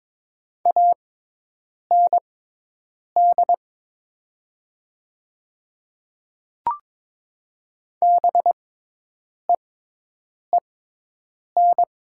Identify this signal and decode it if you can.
Morse code, heard as a keyed tone, about 22 wpm, spaced as for 8 wpm, 700 Hz. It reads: AND BEEN
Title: 6x Character Spacing - Top 100 Words - 22wpm